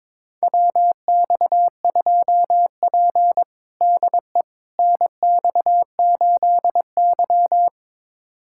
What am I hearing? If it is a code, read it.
Morse code, heard as a keyed tone, about 22 wpm, 700 Hz. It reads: WX2P DE NX8Y
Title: USA Callsign DE Callsign - 4 Characters - 22wpm